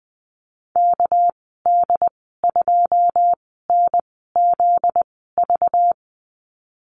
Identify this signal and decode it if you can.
Morse code, heard as a keyed tone, about 20 wpm, spaced as for 14 wpm, 700 Hz. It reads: KD2NZV
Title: 245 Lesson 39g - Letter z - Call Signs 20wpm